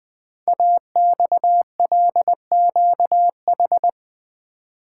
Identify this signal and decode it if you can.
Morse code, heard as a keyed tone, about 20 wpm, 700 Hz. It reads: AXLQH
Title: Code Groups - 5 Letters - 20wpm